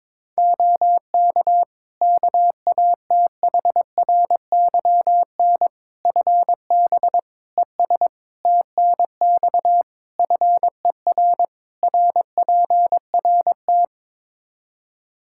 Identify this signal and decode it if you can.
Morse code, heard as a keyed tone, about 22 wpm, 700 Hz. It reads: OK KATHRYN FB ES TNX FER RPRT
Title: K1USN SST QSO Practice 22wpm